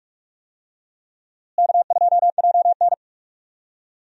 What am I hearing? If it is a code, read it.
Morse code, heard as a keyed tone, about 46 wpm, 700 Hz. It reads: K2JN
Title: USA Call Signs - 4 Characters - Repeated 2x - 45wpm